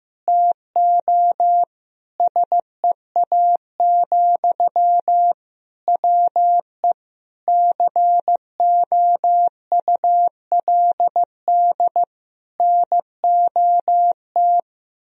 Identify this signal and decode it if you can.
Morse code, heard as a keyed tone, about 16 wpm, 700 Hz. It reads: TO SEA, WE COULD NOT
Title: Sentences from Top 1000 Words - Review Part 2 - 15wpm